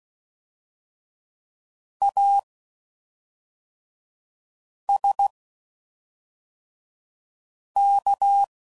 Morse code, heard as a keyed tone, about 16 wpm, spaced as for 3 wpm, 800 Hz. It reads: ASK
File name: Suffixes / Special INGs / Fast INGs